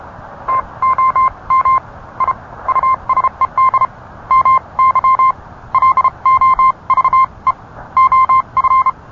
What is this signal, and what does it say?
Morse code, heard as a keyed tone, about 31 wpm, spaced as for 26 wpm, 1050 Hz. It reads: TOM I USED MY LOVE OF